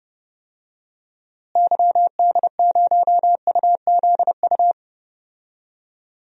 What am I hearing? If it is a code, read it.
Morse code, heard as a keyed tone, about 30 wpm, 700 Hz. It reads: YD0UZU
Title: DXCC-100 Call Signs - 6 Characters - Repeated 3x - 30wpm